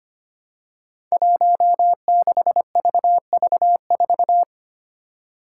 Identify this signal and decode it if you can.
Morse code, heard as a keyed tone, about 25 wpm, 700 Hz. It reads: 16VV4
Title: Five Challenging Letter-Number Combinations 25wpm